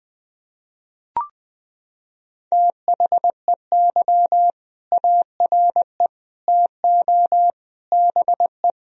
Morse code, heard as a keyed tone, about 20 wpm, 700 Hz. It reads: THEY ARE TO BE
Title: Sentences from Top 1000 Words - Review Part 3 - 20wpm